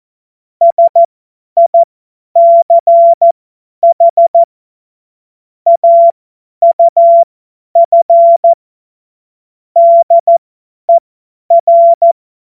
Morse code, heard as a keyed tone, about 14 wpm, spaced as for 10 wpm, 680 Hz. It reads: SICH AUF DER
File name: Morsekurs / Witze / Witz 22. August 2023 50 F70